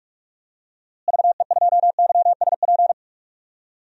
Morse code, heard as a keyed tone, about 43 wpm, 700 Hz. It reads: VE2YSP
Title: DXCC-100 Call Signs - 6 Characters - Repeated 2x - 45wpm